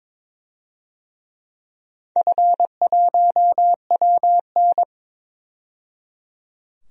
Morse code, heard as a keyed tone, about 22 wpm, 700 Hz. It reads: F1WN
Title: DXCC-100 Call Signs - 4 Characters - 22wpm